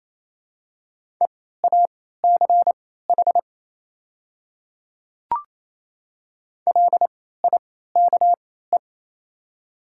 Morse code, heard as a keyed tone, about 27 wpm, spaced as for 16 wpm, 700 Hz. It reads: EACH LIKE
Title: 3x Character Spacing - Top 200 Words - 28wpm